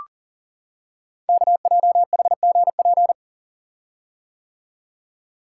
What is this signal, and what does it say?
Morse code, heard as a keyed tone, about 42 wpm, 690 Hz. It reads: KJHGP
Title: Code Groups - 5 Letters - 40wpm